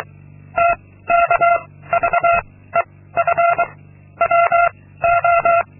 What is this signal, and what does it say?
Morse code, heard as a keyed tone, about 24 wpm, 690 Hz. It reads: T K V E F W O
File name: Linksmakalnis-2